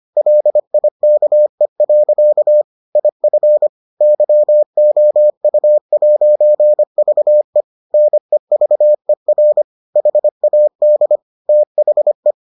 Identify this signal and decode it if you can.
Morse code, heard as a keyed tone, about 25 wpm, 600 Hz. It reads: LIKE. IF YOU'VE NEVER HAD THE